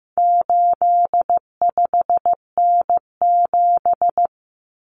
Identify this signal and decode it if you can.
Morse code, heard as a keyed tone, about 15 wpm, 700 Hz. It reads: 85N7
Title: Common Cut Numbers - 4-Digit 15wpm